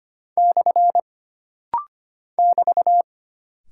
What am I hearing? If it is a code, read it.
Morse code, heard as a keyed tone, about 25 wpm, 700 Hz. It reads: / <BT>
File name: Single Punctuation-Prosign - Rapid-Fire - 25wpm